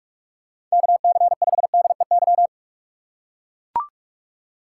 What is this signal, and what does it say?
Morse code, heard as a keyed tone, about 41 wpm, 700 Hz. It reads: KC5DEY
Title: USA Call Signs - 6 Characters - 4x Character Spacing - 45wpm